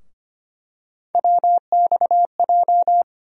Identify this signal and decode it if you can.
Morse code, heard as a keyed tone, about 25 wpm, 700 Hz. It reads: WXJ
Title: USA Call Sign Postfixes - 3 Letters 25wpm